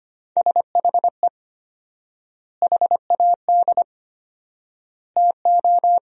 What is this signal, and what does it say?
Morse code, heard as a keyed tone, about 25 wpm, 700 Hz. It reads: SHE HAD TO
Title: 4x Word-Spacing - Sentences from Top 200 Words - 25wpm